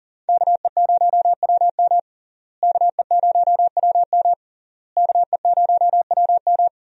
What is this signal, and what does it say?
Morse code, heard as a keyed tone, about 41 wpm, 700 Hz. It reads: KE0WM KE0WM KE0WM
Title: USA Call Signs - 5 Characters - Repeated 3x - 40wpm